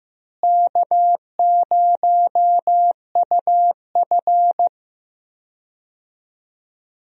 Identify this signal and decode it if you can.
Morse code, heard as a keyed tone, about 15 wpm, 700 Hz. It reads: K0UF
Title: USA Call Signs - 4 Characters - Repeated 3x - 15wpm